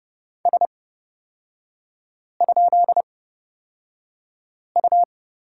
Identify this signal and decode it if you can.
Morse code, heard as a keyed tone, about 30 wpm, spaced as for 5 wpm, 700 Hz. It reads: S?U
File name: Lesson 38e - Letter q - Three Characters 30wpm